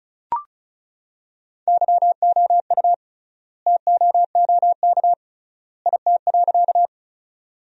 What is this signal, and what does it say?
Morse code, heard as a keyed tone, about 36 wpm, 700 Hz. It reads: YOU TOOK IT.
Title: 3x Word-Spacing - Sentences from Top 300 Words - 35wpm